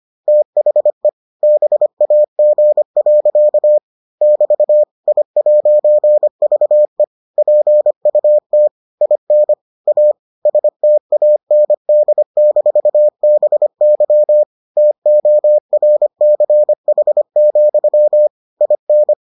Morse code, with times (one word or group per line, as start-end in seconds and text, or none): 0.3-1.1 THE
1.4-3.8 BAG.
4.2-4.8 <BT>
5.1-7.0 I'VE
7.4-8.7 PUT
9.0-9.5 IN
9.9-10.1 A
10.5-14.4 STAND-BY
14.8-18.3 TORCH,
18.6-19.1 IN